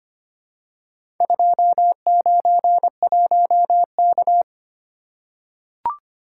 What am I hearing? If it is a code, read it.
Morse code, heard as a keyed tone, about 25 wpm, 700 Hz. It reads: 291K
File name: practice / 25wpm / Four Letter-Number Combinations - 25wpm